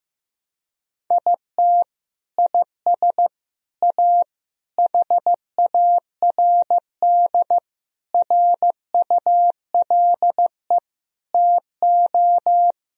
Morse code, heard as a keyed tone, about 15 wpm, 700 Hz. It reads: IT IS A HARD RULE TO
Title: Sentences from Top 1000 Words - Review Part 2 - 15wpm